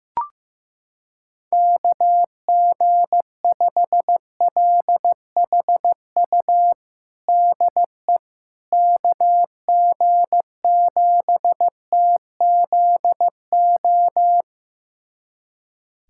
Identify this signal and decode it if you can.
Morse code, heard as a keyed tone, about 15 wpm, 700 Hz. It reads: KG5LHU DE KG7TZO
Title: USA Callsign DE Callsign - 6 Characters - 15wpm